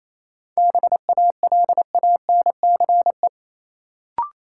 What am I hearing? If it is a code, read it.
Morse code, heard as a keyed tone, about 28 wpm, 700 Hz. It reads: BALANCE
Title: Sentences from Top 900 Words - New Word Drill - 28wpm